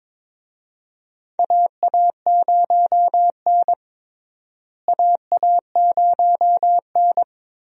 Morse code, heard as a keyed tone, about 22 wpm, 700 Hz. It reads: AA0N AA0N
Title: USA Call Signs - 4 Characters - Repeated 2x - 22wpm